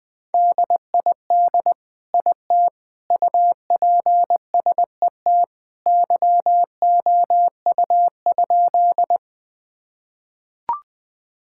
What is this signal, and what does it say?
Morse code, heard as a keyed tone, about 20 wpm, 700 Hz. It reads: DID IT UPSET YOU?
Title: Sentences from Top 1000 Words - Review Part 5 - 20wpm